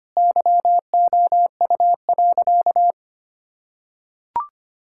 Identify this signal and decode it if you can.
Morse code, heard as a keyed tone, about 25 wpm, 700 Hz. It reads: YOU.
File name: practice / 25wpm / Sentences with 3-Word Phrases - 25wpm